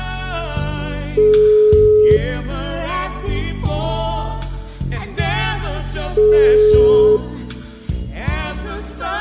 Ringback tone (Europe, 425 Hz).